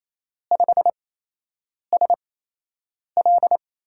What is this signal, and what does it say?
Morse code, heard as a keyed tone, about 27 wpm, spaced as for 8 wpm, 700 Hz. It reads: 5SL
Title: USA Call Signs - 5 Characters - 8x Character Spacing - 28wpm